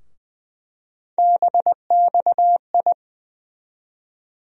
Morse code, heard as a keyed tone, about 20 wpm, 700 Hz. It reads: BXI
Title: USA Call Sign Postfixes - 3 Letters 20wpm